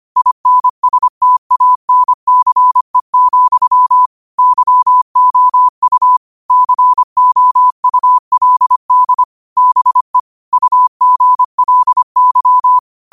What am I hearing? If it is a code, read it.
Morse code, heard as a keyed tone, about 25 wpm, 1000 Hz. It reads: INSTANCE, YOU COULD BE UGLY